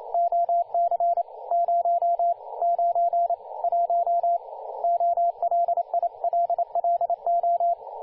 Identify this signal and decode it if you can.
Morse code, heard as a keyed tone, about 29 wpm, spaced as for 25 wpm, 690 Hz, with radio noise. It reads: OC 091 OLILLO
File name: du1-ja1pbv-1-28-040mhz-cw-cqing